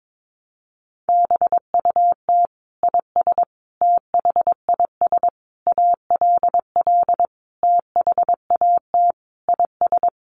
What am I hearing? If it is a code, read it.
Morse code, heard as a keyed tone, about 22 wpm, 700 Hz. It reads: BUT IS THIS ALL THAT IS
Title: Sentences from Top 600 Words 22wpm